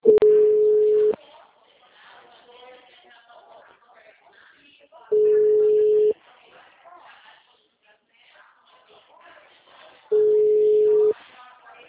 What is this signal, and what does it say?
Ringback tone (Europe, 425 Hz), with speech.